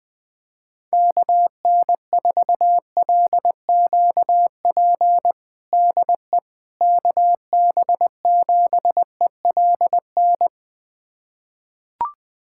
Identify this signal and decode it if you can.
Morse code, heard as a keyed tone, about 20 wpm, 700 Hz. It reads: KN4LQP DE KB7ELN
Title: USA Callsign DE Callsign 20wpm